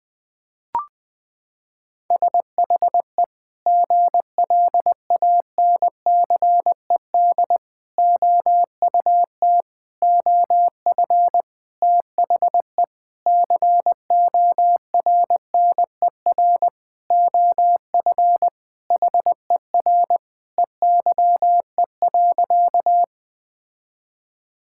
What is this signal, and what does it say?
Morse code, heard as a keyed tone, about 20 wpm, 700 Hz. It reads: SHE GLANCED OUT OF THE CORNER OF HER EYE.
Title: Sentences with 5-Word Phrases - 20wpm